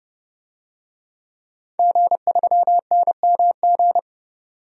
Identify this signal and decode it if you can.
Morse code, heard as a keyed tone, about 30 wpm, 700 Hz. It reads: G3NMG
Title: DXCC-100 Call Signs - 5 Characters - 30wpm